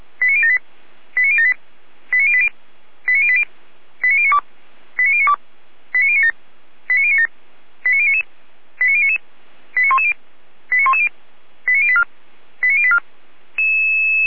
Five-tone selective call (ZVEI), heard as quick runs of five tones, tones of about 70 ms. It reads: 89078 89078 89080 89080 89092 89092 89097 89097 89099 89099 89110 89110 89084 89084